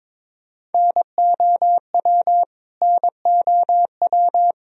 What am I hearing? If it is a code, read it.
Morse code, heard as a keyed tone, about 21 wpm, 700 Hz. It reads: NOW NOW
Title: Top 100 Words Repeated 22wpm